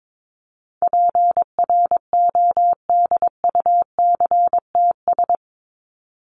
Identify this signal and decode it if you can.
Morse code, heard as a keyed tone, about 22 wpm, 700 Hz. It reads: PRODUCTS
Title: ES:S Suffix - Top 100 Words with Base Words 22wpm